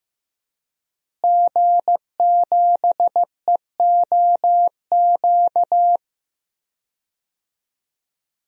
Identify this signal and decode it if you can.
Morse code, heard as a keyed tone, about 15 wpm, 700 Hz. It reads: G7EOQ
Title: DXCC-100 Call Signs - 5 Characters - Repeated 2x - 15wpm